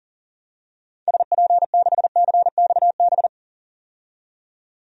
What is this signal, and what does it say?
Morse code, heard as a keyed tone, about 40 wpm, 700 Hz. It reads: SP6CXB
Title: DXCC-100 Call Signs - 6 Characters - 40wpm